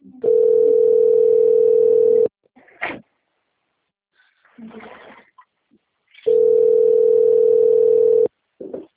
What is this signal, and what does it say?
Ringback tone (North America), 440 + 480 Hz.
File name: rg-1600-unknown- - - .11786